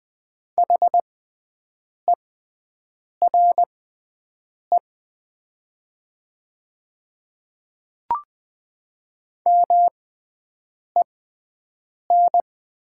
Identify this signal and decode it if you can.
Morse code, heard as a keyed tone, about 20 wpm, spaced as for 7 wpm, 690 Hz. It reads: HERE MEN
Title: 6x Character Spacing - Top 200 Words - 20wpm